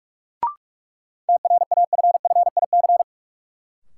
Morse code, heard as a keyed tone, about 46 wpm, 700 Hz. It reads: TRAFFIC
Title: Sentences from Top 1300 Words - New Word Drill - 45wpm